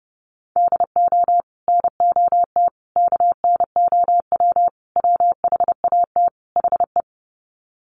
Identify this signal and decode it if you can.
Morse code, heard as a keyed tone, about 28 wpm, 700 Hz. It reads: DO NOT KNOW WHAT HE
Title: Sets of 5 Words Encore2 30wpm